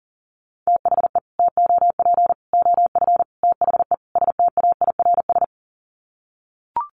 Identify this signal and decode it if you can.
Morse code, heard as a keyed tone, about 40 wpm, 700 Hz. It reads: THE TOP OF THE STAIRS
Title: Sets of 5 Words Encore 40wpm